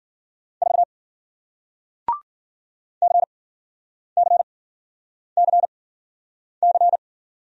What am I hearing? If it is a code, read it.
Morse code, heard as a keyed tone, about 54 wpm, spaced as for 9 wpm, 700 Hz. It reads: 4 CCCC